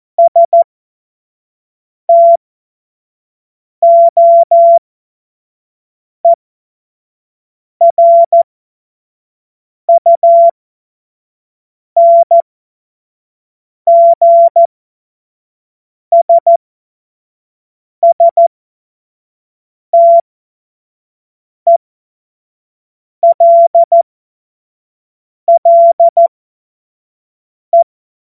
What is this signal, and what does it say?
Morse code, heard as a keyed tone, about 14 wpm, spaced as for 5 wpm, 680 Hz. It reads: STOERUNGSSTELLE
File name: Morsekurs / Witze / Witz 20 Februar 2024 25 F70